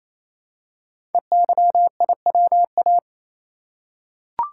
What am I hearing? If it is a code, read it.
Morse code, heard as a keyed tone, about 28 wpm, 700 Hz. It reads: EYIWA
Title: Code Groups - 5 Letters - 28wpm